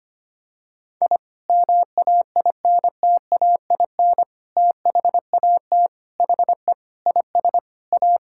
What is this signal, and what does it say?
Morse code, heard as a keyed tone, about 25 wpm, 700 Hz. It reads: I MAINTAIN THAT HE IS A
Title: Sentences from Top 1000 Words - Review Part 5 - 25wpm